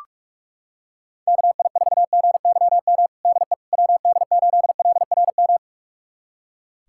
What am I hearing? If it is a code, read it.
Morse code, heard as a keyed tone, about 46 wpm, 700 Hz. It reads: KI4GYM DE WD8LRM
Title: USA Callsign DE Callsign - 6 Characters - 45wpm